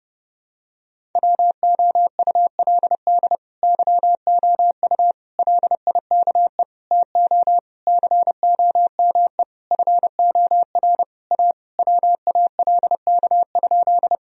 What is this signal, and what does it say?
Morse code, heard as a keyed tone, about 32 wpm, 700 Hz. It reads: WOULD YOU LIKE TO COME FOR A WALK?
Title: Sentences from Top 1000 Words - Review Part 1 - 30wpm